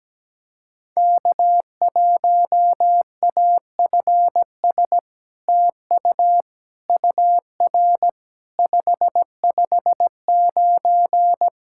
Morse code, heard as a keyed tone, about 17 wpm, 700 Hz. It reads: K1AFS TU UR 559
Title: SOTA QSO Practice 17wpm